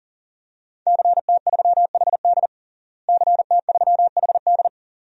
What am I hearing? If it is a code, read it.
Morse code, heard as a keyed tone, about 37 wpm, 700 Hz. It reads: CT3HD CT3HD
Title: DXCC-100 Call Signs - 5 Characters - Repeated 2x - 40wpm